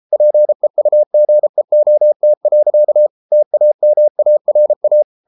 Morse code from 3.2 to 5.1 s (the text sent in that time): TAMARA